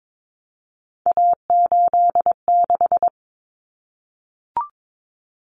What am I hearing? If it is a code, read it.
Morse code, heard as a keyed tone, about 22 wpm, 700 Hz. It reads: A86
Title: Common Cut Numbers - 3-Digit 22wpm